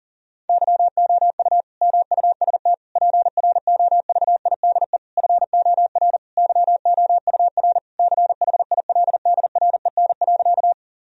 Morse code, heard as a keyed tone, about 40 wpm, 700 Hz. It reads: YOU MUST PROVIDE FOR YOUR CHILDREN.
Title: Sentences from Top 1000 Words - Review Part 3 - 40wpm